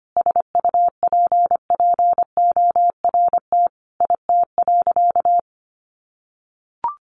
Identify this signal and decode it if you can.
Morse code, heard as a keyed tone, about 25 wpm, 700 Hz. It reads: SUPPORT IT.
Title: Sentences from Top 1300 Words 25wpm